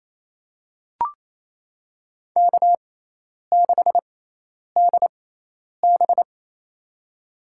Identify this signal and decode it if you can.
Morse code, heard as a keyed tone, about 28 wpm, spaced as for 10 wpm, 700 Hz. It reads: K6DB
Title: USA Call Signs - 4 Characters - 6x Character Spacing - 28wpm